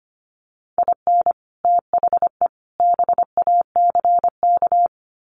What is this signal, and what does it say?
Morse code, heard as a keyed tone, about 25 wpm, 700 Hz. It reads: IN THE BACK
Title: Sets of 3 Words 25wpm